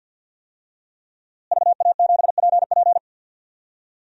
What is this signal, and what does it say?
Morse code, heard as a keyed tone, about 51 wpm, 700 Hz. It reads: VA7PP
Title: DXCC-100 Call Signs - 5 Characters - Repeated 3x - 50wpm